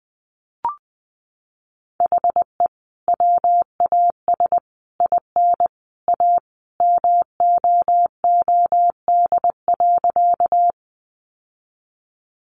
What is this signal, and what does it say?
Morse code, heard as a keyed tone, about 19 wpm, 700 Hz. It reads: HE WAS IN A MOOD.
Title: Sentences from Top 1100 Words 20wpm